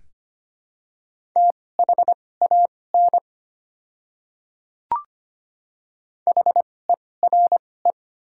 Morse code, heard as a keyed tone, about 25 wpm, spaced as for 18 wpm, 700 Hz. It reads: THAN HERE
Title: 2x Character Spacing - Top 300 Words - 25wpm